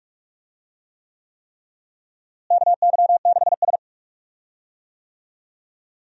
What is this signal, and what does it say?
Morse code, heard as a keyed tone, about 43 wpm, 700 Hz. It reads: KY6S
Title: USA Call Signs - 4 Characters - Repeated 3x - 45wpm